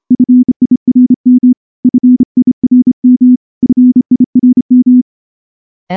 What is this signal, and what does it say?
Morse code, heard as a keyed tone, about 28 wpm, 250 Hz. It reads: FIRM FIRM FIRM
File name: morse-training-30WPM